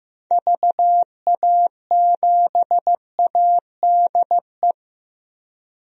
Morse code, heard as a keyed tone, about 16 wpm, 700 Hz. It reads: VA7ADE
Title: DXCC-100 Call Signs - 6 Characters - Repeated 2x - 15wpm